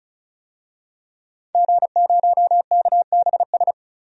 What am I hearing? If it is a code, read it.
Morse code, heard as a keyed tone, about 34 wpm, 700 Hz. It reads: G0KBS
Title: DXCC-100 Call Signs - 5 Characters - 35wpm